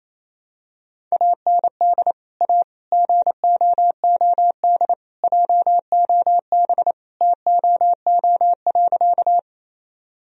Morse code, heard as a keyed tone, about 27 wpm, 700 Hz. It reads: AND A GOOD JOB TOO.
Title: Sentences from Top 1000 Words - Review Part 3 - 28wpm